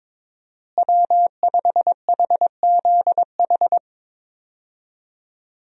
Morse code, heard as a keyed tone, about 22 wpm, 700 Hz. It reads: W5HZH
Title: USA Call Signs - 5 Characters - Repeated 3x - 22wpm